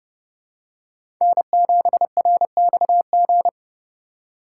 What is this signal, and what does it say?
Morse code, heard as a keyed tone, about 30 wpm, 700 Hz. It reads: N7RXG
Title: USA Call Signs - 5 Characters - 2x Character Spacing - 30wpm